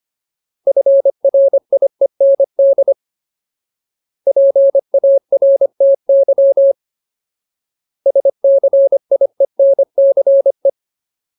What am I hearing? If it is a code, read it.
Morse code, heard as a keyed tone, about 25 wpm, 550 Hz. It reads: FRIEND PARTY SCIENCE
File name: Ultimate Word List / 25WPM Ultimate